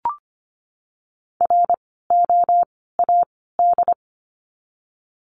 Morse code, heard as a keyed tone, about 25 wpm, 700 Hz. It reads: R O A D
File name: course / 25:10wpm / 094 Lesson 16f - Letter c - Words 25wpm